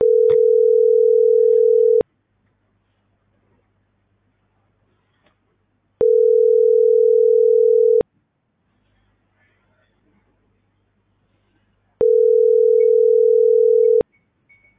Ringback tone (North America), 440 + 480 Hz.